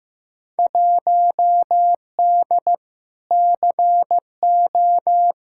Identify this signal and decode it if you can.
Morse code, heard as a keyed tone, about 15 wpm, 700 Hz. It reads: 1D CO
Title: Field Day Exchanges - 15wpm